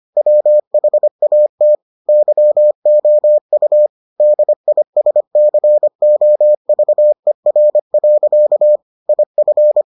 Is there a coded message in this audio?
Yes: Morse code from 0.2 to 9.8 s, heard as a keyed tone, about 25 wpm, 600 Hz. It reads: WHAT YOU DISCOVER. IF